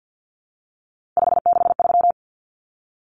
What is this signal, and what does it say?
Morse code, heard as a keyed tone, about 50 wpm, 700 Hz. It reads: 563
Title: Common Cut Numbers - 3-Digit 50wpm